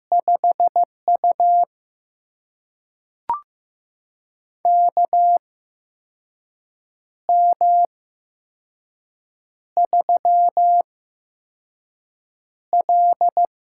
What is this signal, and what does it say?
Morse code, heard as a keyed tone, about 15 wpm, spaced as for 5 wpm, 700 Hz. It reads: 5U KM3L
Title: USA Call Signs - 4 Characters - 8x Character Spacing - 15wpm